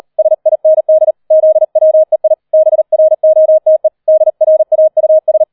Morse code, heard as a keyed tone, about 38 wpm, 630 Hz. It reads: SIND ZWEI BROTE DRAUS